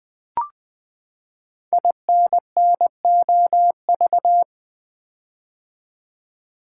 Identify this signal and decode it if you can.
Morse code, heard as a keyed tone, about 20 wpm, 700 Hz. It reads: INNOV
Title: Build A Word - 5 Letters At A Time - 10-Letter Words - 20wpm